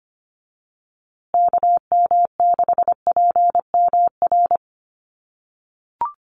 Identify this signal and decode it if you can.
Morse code, heard as a keyed tone, about 25 wpm, 700 Hz. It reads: KM6PMR
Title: USA Call Signs Repeated 2x 25wpm